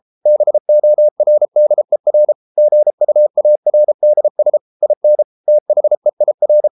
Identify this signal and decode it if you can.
Morse code, heard as a keyed tone, about 33 wpm, 600 Hz. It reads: BORDER GUARDS IN THEIR